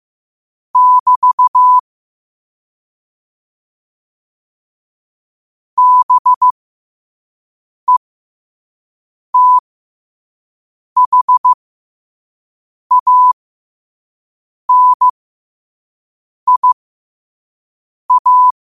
Morse code, heard as a keyed tone, about 15 wpm, spaced as for 5 wpm, 1000 Hz. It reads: <BT> BETHANIA
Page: Quotes for Wed, 13 Aug 2025 in Morse Code at 5 words per minute.